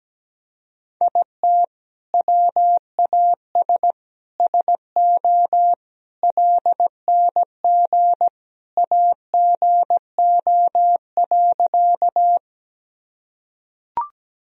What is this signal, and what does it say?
Morse code, heard as a keyed tone, about 17 wpm, 700 Hz. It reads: IT WAS SO LNG AGO.
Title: Sentences from Top 500 Words and Abbreviations 17wpm